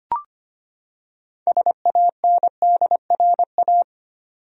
Morse code, heard as a keyed tone, about 25 wpm, 700 Hz. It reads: SANDRA
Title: USA Common Names - 25wpm